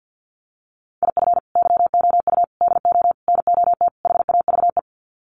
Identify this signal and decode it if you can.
Morse code, heard as a keyed tone, about 50 wpm, 700 Hz. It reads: IF YOU DO NOT HAVE
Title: Sets of 5 Words 50wpm